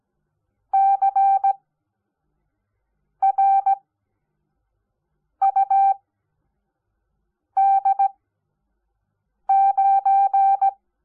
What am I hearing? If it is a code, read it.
Morse code, heard as a keyed tone, about 17 wpm, spaced as for 5 wpm, 780 Hz. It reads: CRUD9
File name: Onlinelehrgang / morsen / DJ4UF Morsen05